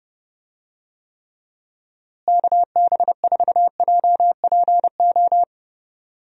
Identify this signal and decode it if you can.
Morse code, heard as a keyed tone, about 30 wpm, 700 Hz. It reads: KB4JPO
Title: USA Call Signs - 6 Characters - Repeated 3x - 30wpm